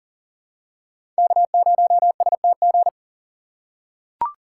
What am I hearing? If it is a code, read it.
Morse code, heard as a keyed tone, about 39 wpm, 700 Hz. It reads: K0STG
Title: USA Call Signs - 5 Characters - 4x Character Spacing - 40wpm